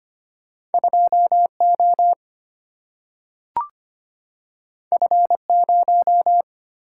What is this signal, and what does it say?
Morse code, heard as a keyed tone, about 25 wpm, 700 Hz. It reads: 2O F0
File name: Two Letter-Number Combinations - 25wpm